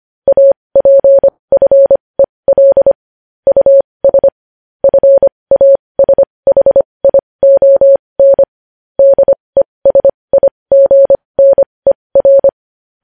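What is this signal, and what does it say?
Morse code, heard as a keyed tone, about 25 wpm, spaced as for 20 wpm, 550 Hz. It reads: APFEL US FASHION DESIGNER
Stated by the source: News headlines sent in Morse code at 20.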